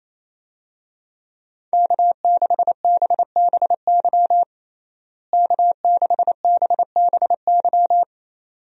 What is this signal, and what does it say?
Morse code, heard as a keyed tone, about 28 wpm, 700 Hz. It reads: K6BBY K6BBY
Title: USA Call Signs - 5 Characters - Repeated 2x - 28wpm